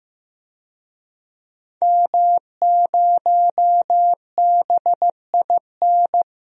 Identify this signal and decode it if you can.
Morse code, heard as a keyed tone, about 15 wpm, 700 Hz. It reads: M0BIN
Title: DXCC-100 Call Signs - 5 Characters - 15wpm